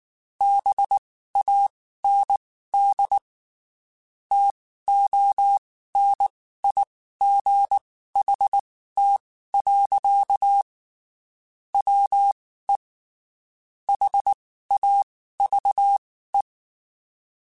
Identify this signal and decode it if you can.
Morse code, heard as a keyed tone, about 19 wpm, spaced as for 14 wpm, 800 Hz. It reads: BAND TONIGHT. WE HAVE
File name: (spaced) / QSOs Spaced-mp3s / WD0BGP 1 @18 sp